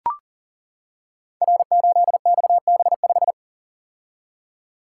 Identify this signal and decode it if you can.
Morse code, heard as a keyed tone, about 39 wpm, 700 Hz. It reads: R8XB5
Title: Code Groups - 5 Letter:Numbers - 40wpm